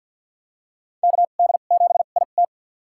Morse code, heard as a keyed tone, about 50 wpm, spaced as for 36 wpm, 700 Hz. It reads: KD7IT
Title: USA Call Signs - 2x Character Spacing - 50wpm